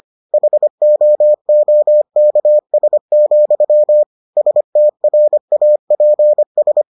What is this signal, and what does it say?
Morse code, heard as a keyed tone, about 25 wpm, 600 Hz. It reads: HOOKS, STRAPS